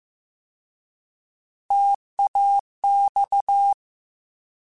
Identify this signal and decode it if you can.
Morse code, heard as a keyed tone, about 15 wpm, 800 Hz. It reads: TAX
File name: T Prefixes / T mp3s